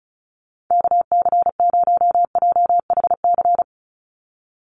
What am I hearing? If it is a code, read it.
Morse code, heard as a keyed tone, about 34 wpm, 700 Hz. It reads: KC0JHC
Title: USA Call Signs 35wpm